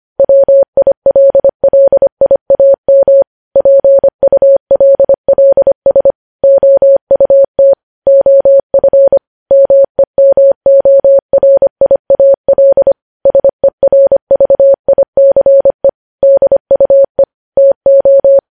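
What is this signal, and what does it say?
Morse code, heard as a keyed tone, about 25 wpm, 550 Hz. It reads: WILLIAM PULLS OUT OF MEMORIAL SERVICE DUE TO
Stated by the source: News headlines sent in Morse code at 25 wpm.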